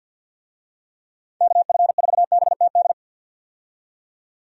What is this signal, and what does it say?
Morse code, heard as a keyed tone, about 46 wpm, 700 Hz. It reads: KF4BTD
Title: USA Call Signs - 6 Characters - 50wpm